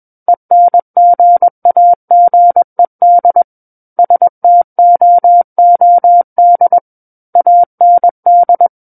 Morse code, heard as a keyed tone, about 21 wpm, 700 Hz. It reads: ENGAGED STOOD AND
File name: Book One / 12-Chapter 12 - 21 WPM